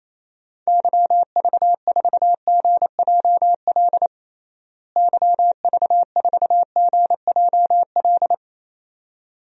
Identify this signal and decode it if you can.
Morse code, heard as a keyed tone, about 28 wpm, 700 Hz. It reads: YV4GJL YV4GJL
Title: DXCC-100 Call Signs - 6 Characters - Repeated 2x - 28wpm